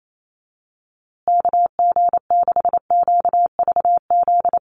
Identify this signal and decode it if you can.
Morse code, heard as a keyed tone, about 28 wpm, 700 Hz. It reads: KG6QVZ
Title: USA Call Signs Repeated 3x 28wpm